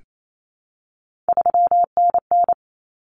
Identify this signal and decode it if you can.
Morse code, heard as a keyed tone, about 28 wpm, 700 Hz. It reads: 3NN
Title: Common Cut Numbers - 3-Digit 28wpm